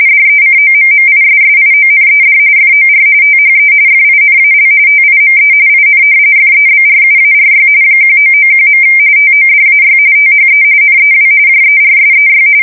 RTTY, about 45.45 baud. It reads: DECOMMISSIONED FOR THE FINAL TIME IN FEBRUARY 1991. TODAY THE NEW JERSEY
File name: fsk170 / battleshipnj